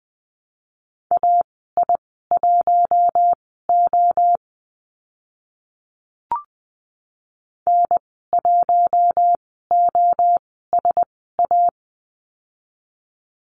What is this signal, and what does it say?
Morse code, heard as a keyed tone, about 20 wpm, spaced as for 14 wpm, 700 Hz. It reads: AI1O N1OSA
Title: 042 Lesson 8g - Number 1 - Call Signs 20wpm